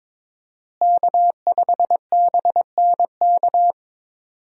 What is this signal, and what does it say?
Morse code, heard as a keyed tone, about 22 wpm, 700 Hz. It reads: K5BNK
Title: USA Call Signs - 5 Characters - Repeated 3x - 22wpm